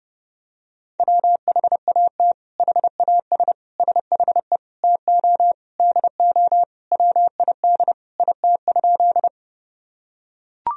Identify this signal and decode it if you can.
Morse code, heard as a keyed tone, about 30 wpm, 700 Hz. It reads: WHAT HAS SHE TO DO WID IT?
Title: Sentences from Top 100 Words and Abbreviations 30wpm